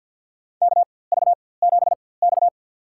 Morse code, heard as a keyed tone, about 51 wpm, spaced as for 23 wpm, 700 Hz. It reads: KV7X
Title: USA Call Signs - 4x Character Spacing - 50wpm